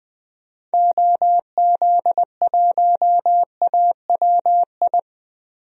Morse code, heard as a keyed tone, about 21 wpm, 700 Hz. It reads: OZ1AWI
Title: DXCC-100 Call Signs - 6 Characters - Repeated 3x - 20wpm